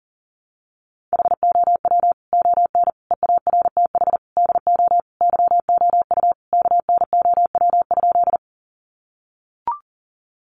Morse code, heard as a keyed tone, about 40 wpm, 700 Hz. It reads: HOW ON EARTH DO YOU KNOW?
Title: Sentences from Top 200 Words 40wpm